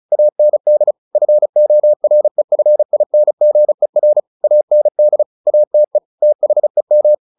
Morse code, heard as a keyed tone, about 34 wpm, 600 Hz. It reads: AND FOREFINGER AND ATE THEM
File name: the old man and the sea 2 35wpm